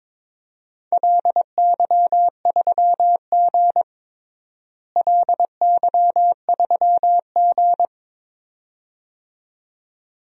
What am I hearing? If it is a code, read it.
Morse code, heard as a keyed tone, about 22 wpm, 700 Hz. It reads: LY3G LY3G
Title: DXCC-100 Call Signs - 4 Characters - Repeated 2x - 22wpm